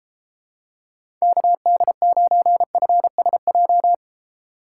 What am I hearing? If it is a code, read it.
Morse code, heard as a keyed tone, about 32 wpm, 700 Hz. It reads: KD9FSJ